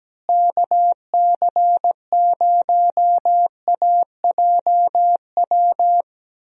Morse code, heard as a keyed tone, about 17 wpm, 700 Hz. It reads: KC0AJW
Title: USA Call Signs - 6 Characters - Repeated 3x - 17wpm